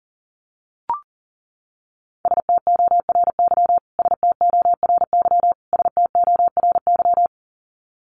Morse code, heard as a keyed tone, about 40 wpm, 700 Hz. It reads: STORY STORY STORY
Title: Top 500 Words Repeated 40wpm